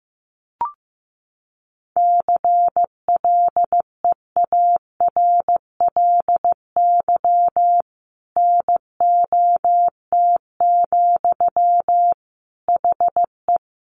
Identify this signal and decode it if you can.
Morse code, heard as a keyed tone, about 15 wpm, 700 Hz. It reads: CLEARLY NOT, HE
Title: Sentences from Top 1300 Words 15wpm